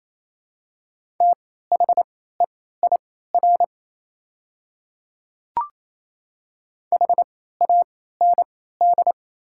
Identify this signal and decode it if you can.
Morse code, heard as a keyed tone, about 28 wpm, spaced as for 16 wpm, 700 Hz. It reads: THEIR HAND
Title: 3x Character Spacing - Top 300 Words - 28wpm